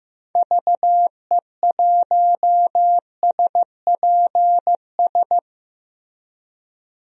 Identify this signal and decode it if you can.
Morse code, heard as a keyed tone, about 15 wpm, 700 Hz. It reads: VE1SPS
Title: DXCC-100 Call Signs - 6 Characters - Repeated 2x - 15wpm